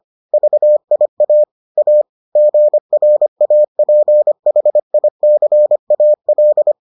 Morse code, heard as a keyed tone, about 25 wpm, 600 Hz. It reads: VIA A GRAPHICAL